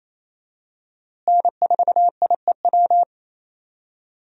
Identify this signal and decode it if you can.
Morse code, heard as a keyed tone, about 28 wpm, 700 Hz. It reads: N4IEW
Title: USA Call Signs - 5 Characters - Repeated 2x - 28wpm